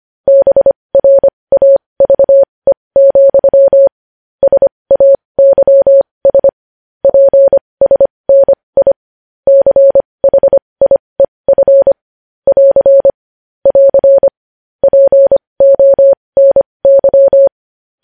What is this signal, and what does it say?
Morse code, heard as a keyed tone, about 25 wpm, spaced as for 20 wpm, 550 Hz. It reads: BRAVE, SAYS PSNI CHIEF <AR> <AR> PONY
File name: podcast / News headlines-1x-25x20